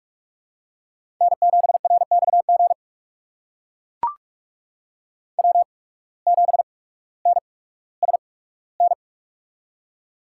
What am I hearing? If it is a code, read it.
Morse code, heard as a keyed tone, about 44 wpm, spaced as for 19 wpm, 700 Hz. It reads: N7RXG W7NSN
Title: USA Call Signs - 5 Characters - 8x Character Spacing - 45wpm